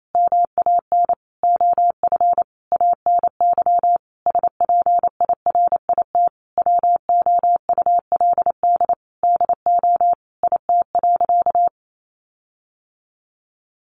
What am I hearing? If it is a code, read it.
Morse code, heard as a keyed tone, about 28 wpm, 700 Hz. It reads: MAN OF ANY SPIRIT WOULD DO IT.
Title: Sentences from Top 1400 Words 28wpm